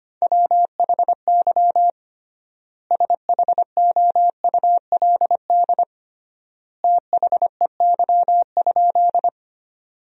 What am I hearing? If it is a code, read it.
Morse code, heard as a keyed tone, about 25 wpm, 700 Hz. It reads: WHY SHOULD THEY?
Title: 3x Word-Spacing - Sentences from Top 300 Words - 25wpm